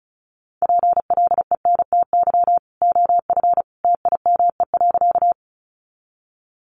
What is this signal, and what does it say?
Morse code, heard as a keyed tone, about 34 wpm, 700 Hz. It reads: PLENTY OF TIME.
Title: Sentences from Top 1300 Words 35wpm